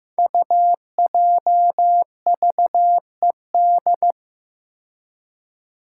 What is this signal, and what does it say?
Morse code, heard as a keyed tone, about 15 wpm, 700 Hz. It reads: UJVED
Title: Five Challenging Letter-Number Combinations 15wpm